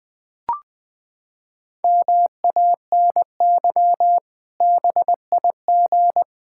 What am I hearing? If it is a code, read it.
Morse code, heard as a keyed tone, about 20 wpm, 700 Hz. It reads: MANY BIG
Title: Pangram Challenge v2 - 20wpm